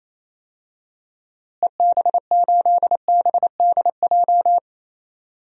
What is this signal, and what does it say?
Morse code, heard as a keyed tone, about 28 wpm, 700 Hz. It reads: EB8BDJ
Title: DXCC-100 Call Signs - 6 Characters - Repeated 3x - 28wpm